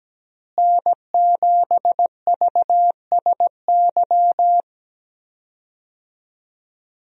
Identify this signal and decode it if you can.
Morse code, heard as a keyed tone, about 17 wpm, 700 Hz. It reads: N7VSY
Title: USA Call Signs - 5 Characters - Repeated 3x - 17wpm